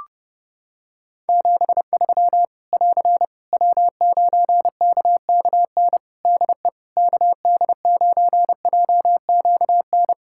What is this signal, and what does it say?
Morse code, heard as a keyed tone, about 31 wpm, 700 Hz. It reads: 73 <AR> W9KKN DE KD9JQN
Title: Practice QSO - Exchange 4 - Ending - 30wpm